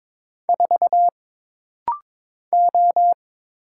Morse code, heard as a keyed tone, about 22 wpm, 700 Hz. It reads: 4 O
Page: This rapid-fire timing is geared to push your ICR (Instant Character Recognition) skills.